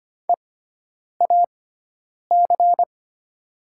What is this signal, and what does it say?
Morse code, heard as a keyed tone, about 25 wpm, spaced as for 9 wpm, 690 Hz. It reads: EAC
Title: 6x Character Spacing - Top 200 Words - 25wpm